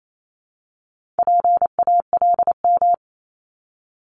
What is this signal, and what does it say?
Morse code, heard as a keyed tone, about 28 wpm, 700 Hz. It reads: PALM
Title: Multi-letter Phonemes Repeated 3x 28wpm